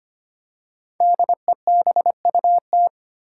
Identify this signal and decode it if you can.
Morse code, heard as a keyed tone, about 25 wpm, 700 Hz. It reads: DEBUT
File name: 5 Letter Words - First 3 Letters Given - 25wpm